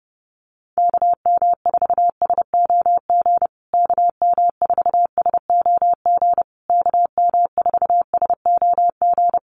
Morse code, heard as a keyed tone, about 28 wpm, 700 Hz. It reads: KM4SOG KM4SOG KM4SOG
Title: USA Call Signs Repeated 3x 30wpm